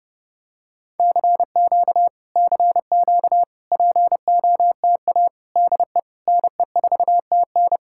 Morse code, heard as a keyed tone, about 32 wpm, 700 Hz. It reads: CQ CQ POTA DE NE4TN